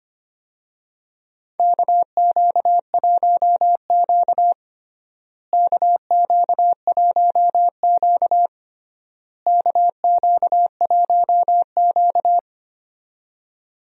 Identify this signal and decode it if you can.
Morse code, heard as a keyed tone, about 25 wpm, 700 Hz. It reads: KQ1Q KQ1Q KQ1Q
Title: USA Call Signs - 4 Characters - Repeated 3x - 25wpm